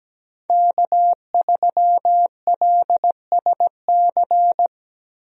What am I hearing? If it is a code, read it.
Morse code, heard as a keyed tone, about 17 wpm, 700 Hz. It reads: K3LSC
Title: USA Call Signs - 5 Characters - Repeated 3x - 17wpm